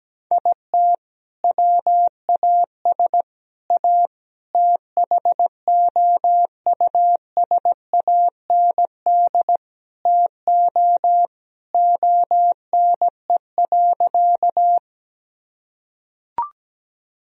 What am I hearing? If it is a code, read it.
Morse code, heard as a keyed tone, about 17 wpm, 700 Hz. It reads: IT WAS A THOUSAND TO ONE.
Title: Sentences from Top 1000 Words - Review Part 3 - 17wpm